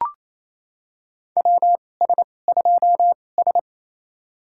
Morse code, heard as a keyed tone, about 28 wpm, spaced as for 20 wpm, 700 Hz. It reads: WS2S
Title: USA Call Signs - 4 Characters - 2x Character Spacing - 28wpm